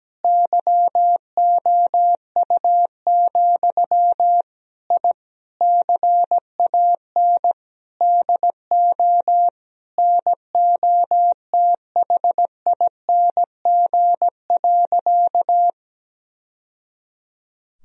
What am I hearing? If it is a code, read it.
Morse code, heard as a keyed tone, about 17 wpm, 700 Hz. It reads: YOU, I CAN DO NOTHING.
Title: Sentences from Top 1000 Words - Review Part 3 - 17wpm